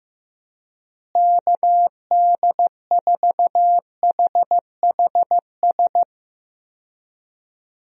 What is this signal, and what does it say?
Morse code, heard as a keyed tone, about 15 wpm, 700 Hz. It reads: KD4HHS
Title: USA Call Signs - 6 Characters - Repeated 2x - 15wpm